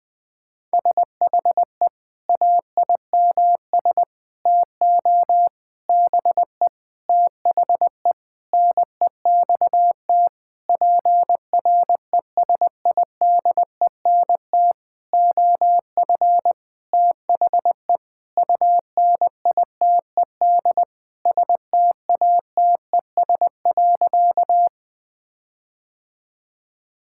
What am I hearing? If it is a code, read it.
Morse code, heard as a keyed tone, about 20 wpm, 700 Hz. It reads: SHE AIMS TO BE THE NEXT PRESIDENT OF THE UNITED STATES.
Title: Sentences with 9-Word Phrases - Spoken Beginning - 20wpm